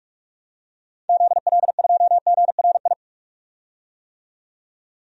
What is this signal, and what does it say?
Morse code, heard as a keyed tone, about 42 wpm, 700 Hz. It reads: ZL2GRI
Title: DXCC-100 Call Signs - 6 Characters - 45wpm